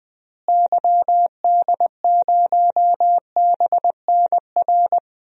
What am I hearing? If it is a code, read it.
Morse code, heard as a keyed tone, about 20 wpm, 700 Hz. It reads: YD0BNR
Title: DXCC-100 Call Signs - 6 Characters - Repeated 3x - 20wpm